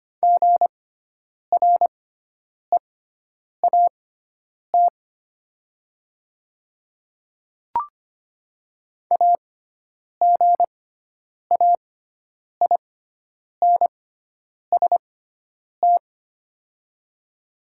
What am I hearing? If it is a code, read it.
Morse code, heard as a keyed tone, about 25 wpm, spaced as for 9 wpm, 700 Hz. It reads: GREAT AGAINST
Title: 6x Character Spacing - Top 500 Words - 25wpm